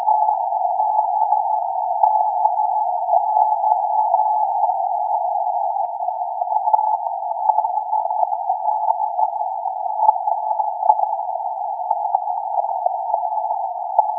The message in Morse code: none